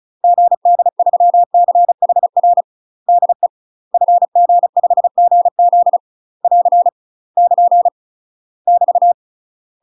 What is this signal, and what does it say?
Morse code, heard as a keyed tone, about 34 wpm, 700 Hz. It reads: GD3CHR DE FG5GZ <AR> <KN> <BT>
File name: UK MASTER QSOs TRAINING / 35WPM / UK MASTER QSOs 0020